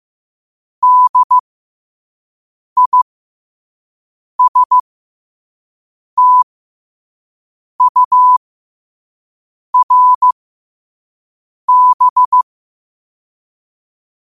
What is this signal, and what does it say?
Morse code, heard as a keyed tone, about 15 wpm, spaced as for 5 wpm, 1000 Hz. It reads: DISTURB
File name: qotdfiles / 5wpm